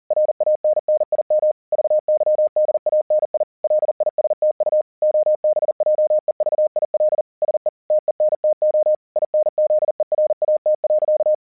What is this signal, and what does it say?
Morse code, heard as a keyed tone, about 40 wpm, 610 Hz. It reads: RANNIM VYDANI LISTU OBJEVIL SE TENTO INZERAT.
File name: vyuka cw 1 / 40wpm / hasek 40wpm uvod